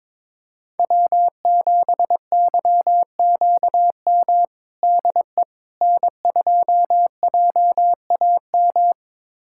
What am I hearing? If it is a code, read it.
Morse code, heard as a keyed tone, about 21 wpm, 700 Hz. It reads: W7YQM DE N2JAM
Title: USA Callsign DE Callsign - 5 Characters - 22wpm